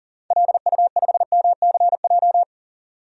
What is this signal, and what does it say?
Morse code, heard as a keyed tone, about 38 wpm, 700 Hz. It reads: LU5MCJ
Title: DXCC-100 Call Signs - 6 Characters - 40wpm